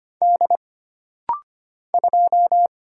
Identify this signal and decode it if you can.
Morse code, heard as a keyed tone, about 25 wpm, 700 Hz. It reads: D 2
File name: Single Character (Letter:Number:Punctuation:Prosign) - Rapid-Fire - 25wpm